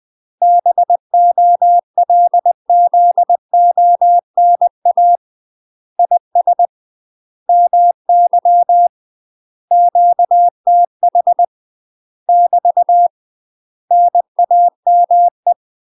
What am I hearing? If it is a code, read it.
Morse code, heard as a keyed tone, about 20 wpm, 700 Hz. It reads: BOLZONA IS MY QTH <BT> NAME
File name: UK MASTER QSOs TRAINING / 20WPM / UK MASTER QSOs 0023